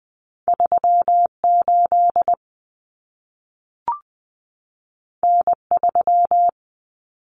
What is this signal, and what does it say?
Morse code, heard as a keyed tone, about 20 wpm, 700 Hz. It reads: 38 N3
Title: Common Cut Numbers - 2-Digit 20wpm